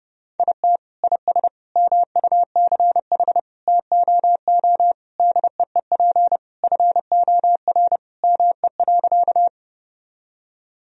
Morse code, heard as a keyed tone, about 30 wpm, 700 Hz. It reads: IT IS MUCH TOO DEEP FOR ME.
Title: Sentences from Top 1000 Words - Review Part 3 - 30wpm